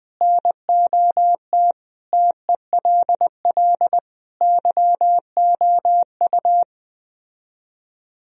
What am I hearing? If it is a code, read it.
Morse code, heard as a keyed tone, about 21 wpm, 700 Hz. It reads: NOT TELL YOU